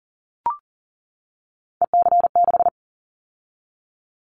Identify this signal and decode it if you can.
Morse code, heard as a keyed tone, about 38 wpm, 700 Hz. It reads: EC6
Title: Callsign Prefixes for Top 50 DXCC - 40wpm